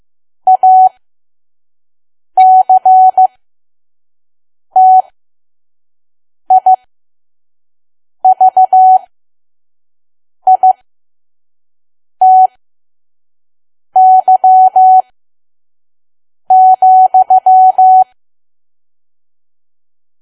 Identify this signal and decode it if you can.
Morse code, heard as a keyed tone, about 15 wpm, spaced as for 5 wpm, 750 Hz. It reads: ACTIVITY,
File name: Archive / 5 WPM files / PM